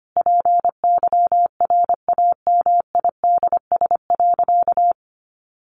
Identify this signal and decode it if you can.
Morse code, heard as a keyed tone, about 25 wpm, 700 Hz. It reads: PYRAMIDS.
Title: A Sound Of Thunder 25wpm